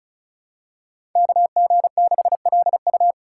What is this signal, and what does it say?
Morse code, heard as a keyed tone, about 36 wpm, 700 Hz. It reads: KG6LU
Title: USA Call Signs - 5 Characters - 3x Character Spacing - 35wpm